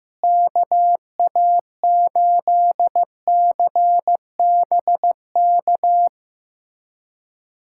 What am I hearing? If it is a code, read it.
Morse code, heard as a keyed tone, about 15 wpm, 700 Hz. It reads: KA8CBK
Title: USA Callsign DE Callsign - 6 Characters - 15wpm